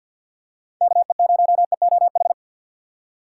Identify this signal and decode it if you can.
Morse code, heard as a keyed tone, about 47 wpm, 700 Hz. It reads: KE0EOH
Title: USA Call Signs - 6 Characters - 6x Character Spacing - 50wpm